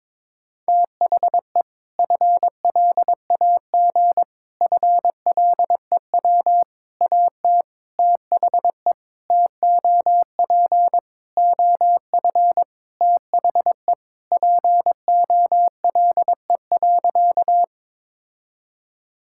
Morse code, heard as a keyed tone, about 22 wpm, 700 Hz. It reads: THE FLAG FLEW AT THE TOP OF THE POLE.
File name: practice / 22wpm / Sentences with 4-Word Phrases - 22wpm